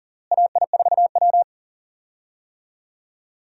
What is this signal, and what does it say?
Morse code, heard as a keyed tone, about 40 wpm, 700 Hz. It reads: AI4W
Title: USA Call Signs - 4 Characters - Repeated 2x - 40wpm